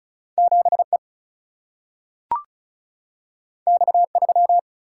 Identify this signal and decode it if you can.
Morse code, heard as a keyed tone, about 36 wpm, 700 Hz. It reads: 7 E X 3
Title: Single-Letter Single-Number 35wpm